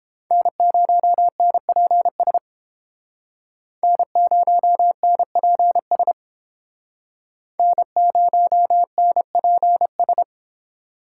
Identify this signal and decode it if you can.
Morse code, heard as a keyed tone, about 29 wpm, 700 Hz. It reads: N0NPS N0NPS N0NPS